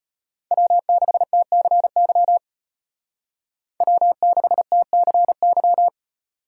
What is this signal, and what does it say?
Morse code, heard as a keyed tone, about 38 wpm, 700 Hz. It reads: W6TCY W6TCY